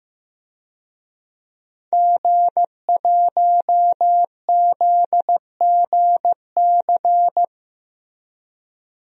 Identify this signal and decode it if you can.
Morse code, heard as a keyed tone, about 15 wpm, 700 Hz. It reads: G1ZGC
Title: DXCC-100 Call Signs - 5 Characters - Repeated 3x - 15wpm